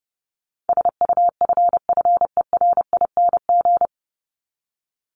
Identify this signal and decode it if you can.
Morse code, heard as a keyed tone, about 30 wpm, 700 Hz. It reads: SUFFERING
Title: ING Suffix - Top 100 Words with Base Words 30wpm